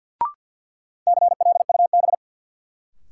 Morse code, heard as a keyed tone, about 52 wpm, 700 Hz. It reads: CLUB
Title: Sentences from Top 1200 Words - New Word Drill - 50wpm